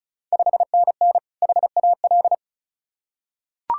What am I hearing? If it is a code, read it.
Morse code, heard as a keyed tone, about 34 wpm, 700 Hz. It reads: 5NN HAL
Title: Ontario QSO Party Exchange 35wpm